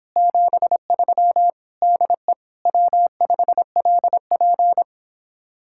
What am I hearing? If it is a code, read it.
Morse code, heard as a keyed tone, about 26 wpm, 700 Hz. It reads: 73 DE W5LP